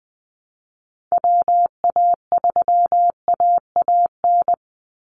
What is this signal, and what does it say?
Morse code, heard as a keyed tone, about 20 wpm, 700 Hz. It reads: WA3AAN
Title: Call Signs Repeated 2x - 20wpm